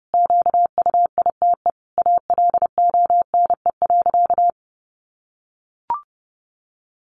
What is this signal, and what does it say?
Morse code, heard as a keyed tone, about 30 wpm, 700 Hz. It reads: QUITE ALONE.
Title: Sentences from Top 1000 Words 30wpm